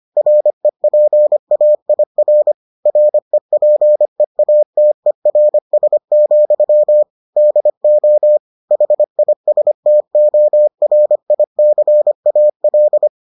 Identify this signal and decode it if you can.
Morse code, heard as a keyed tone, about 25 wpm, 600 Hz. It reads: REPAIR REPEATERS, DO HISTORICAL